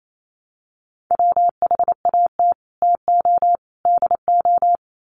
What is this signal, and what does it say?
Morse code, heard as a keyed tone, about 28 wpm, 700 Hz. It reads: WHAT TO DO
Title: Sets of 5 Words Encore2 28wpm